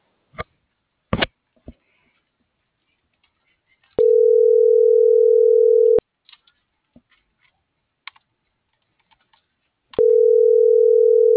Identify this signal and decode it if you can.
Ringback tone (North America), 440 + 480 Hz.